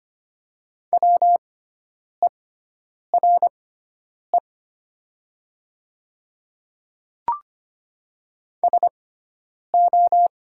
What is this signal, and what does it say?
Morse code, heard as a keyed tone, about 25 wpm, spaced as for 9 wpm, 700 Hz. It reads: WERE SO
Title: 6x Character Spacing - Top 100 Words - 25wpm